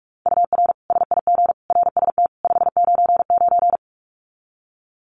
Morse code, heard as a keyed tone, about 44 wpm, 700 Hz. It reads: UR SIG RST 599
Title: Common QSO Phrases 45wpm